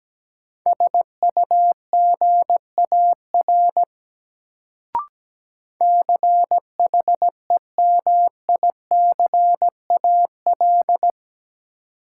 Morse code, heard as a keyed tone, about 17 wpm, 700 Hz. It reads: SUGAR CHEMICAL
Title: Sentences from Top 1200 Words - New Word Drill - 17wpm